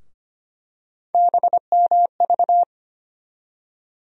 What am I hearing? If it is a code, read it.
Morse code, heard as a keyed tone, about 25 wpm, 700 Hz. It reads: BMV
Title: USA Call Sign Postfixes - 3 Letters 25wpm